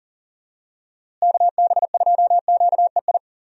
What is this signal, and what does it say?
Morse code, heard as a keyed tone, about 40 wpm, 700 Hz. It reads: KB2QEI
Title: USA Call Signs - 6 Characters - Repeated 2x - 40wpm